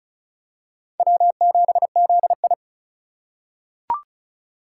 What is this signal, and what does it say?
Morse code, heard as a keyed tone, about 36 wpm, 700 Hz. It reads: W7ZI
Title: USA Call Signs - 4 Characters - 4x Character Spacing - 35wpm